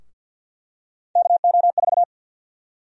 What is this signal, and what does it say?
Morse code, heard as a keyed tone, about 50 wpm, 700 Hz. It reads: KO4
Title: USA Call Sign Prefixes - 50wpm